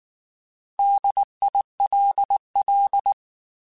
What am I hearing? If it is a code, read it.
Morse code, heard as a keyed tone, about 19 wpm, 800 Hz. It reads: DILL
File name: Alphabetical / Alphabetical mp3s / Ds @18